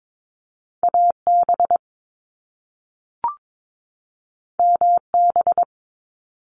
Morse code, heard as a keyed tone, about 22 wpm, 700 Hz. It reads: AB MB
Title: Canadian Provinces and Territories 22wpm